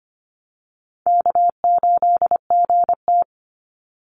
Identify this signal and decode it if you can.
Morse code, heard as a keyed tone, about 25 wpm, 700 Hz. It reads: K8GT
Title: Easy Call Signs Repeated 3x - 25wpm